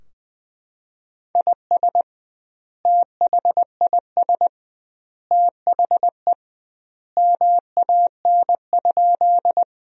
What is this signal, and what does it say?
Morse code, heard as a keyed tone, about 21 wpm, 700 Hz. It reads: IS THIS THE MAN?
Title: 2x Word-Spacing - Sentences from Top 200 Words - 20wpm